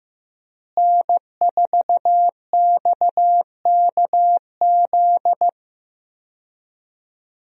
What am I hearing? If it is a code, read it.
Morse code, heard as a keyed tone, about 15 wpm, 700 Hz. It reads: N4XKZ
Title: USA Call Signs - 5 Characters - Repeated 2x - 15wpm